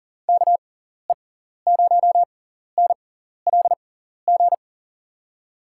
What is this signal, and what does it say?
Morse code, heard as a keyed tone, about 40 wpm, spaced as for 14 wpm, 700 Hz. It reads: KE0NLG
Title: USA Call Signs - 6 Characters - 6x Character Spacing - 40wpm